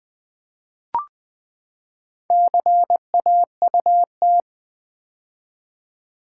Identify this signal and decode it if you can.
Morse code, heard as a keyed tone, about 20 wpm, 700 Hz. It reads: CAUT
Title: Build A Word - 4 Letters At A Time - 8-Letter Words - 20wpm